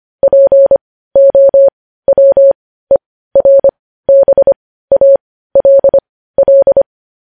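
Morse code, heard as a keyed tone, about 25 wpm, spaced as for 15 wpm, 550 Hz. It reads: POWERBALL
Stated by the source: Morse code news headlines at 15 wpm.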